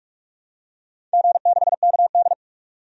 Morse code, heard as a keyed tone, about 44 wpm, 700 Hz. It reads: G6KD
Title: DXCC-100 Call Signs - 4 Characters - 45wpm